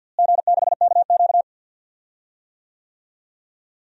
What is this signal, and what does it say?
Morse code, heard as a keyed tone, about 51 wpm, 700 Hz. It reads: G6KQ
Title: DXCC-100 Call Signs - 4 Characters - Repeated 3x - 50wpm